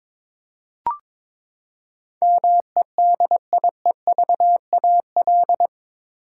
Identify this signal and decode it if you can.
Morse code, heard as a keyed tone, about 21 wpm, 700 Hz. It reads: MEDIEVAL
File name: practice / 22wpm / Eight Letter Words - 22wpm